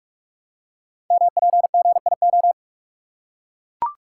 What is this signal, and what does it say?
Morse code, heard as a keyed tone, about 44 wpm, 690 Hz. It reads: MPGIO
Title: Code Groups - 5 Letters - 45wpm